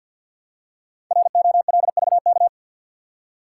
Morse code, heard as a keyed tone, about 53 wpm, 700 Hz. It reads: ROLVK
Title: Code Groups - 5 Letter:Numbers - 50wpm